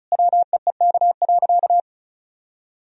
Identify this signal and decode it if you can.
Morse code, heard as a keyed tone, about 36 wpm, 700 Hz. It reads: WEEK.
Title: Sentences from Top 1000 Words - Review Part 2 - 35wpm